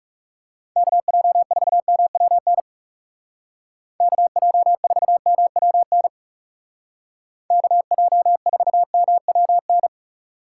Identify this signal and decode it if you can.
Morse code, heard as a keyed tone, about 37 wpm, 700 Hz. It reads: KJ4MWN KJ4MWN KJ4MWN
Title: Speed Racing - USA Call Signs 30wpm